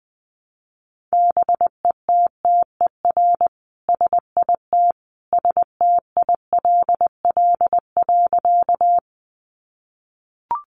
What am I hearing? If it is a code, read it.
Morse code, heard as a keyed tone, about 20 wpm, 700 Hz. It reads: BETTER SIT STILL.
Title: Sentences from Top 500 Words 20wpm